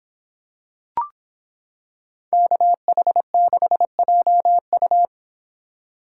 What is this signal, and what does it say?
Morse code, heard as a keyed tone, about 26 wpm, 700 Hz. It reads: KH6JU